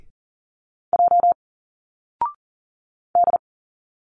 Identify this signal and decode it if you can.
Morse code, heard as a keyed tone, about 40 wpm, 700 Hz. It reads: J D
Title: Single Letters 40wpm